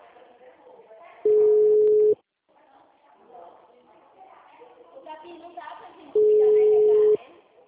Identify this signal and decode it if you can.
Ringback tone (Europe, 425 Hz), with speech.